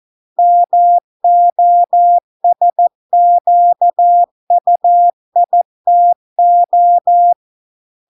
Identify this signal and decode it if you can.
Morse code, heard as a keyed tone, about 14 wpm, 700 Hz. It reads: MOSQUITO